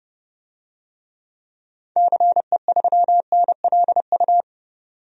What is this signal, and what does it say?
Morse code, heard as a keyed tone, about 32 wpm, 700 Hz. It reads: CE3NLU
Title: DXCC-100 Call Signs - 6 Characters - 30wpm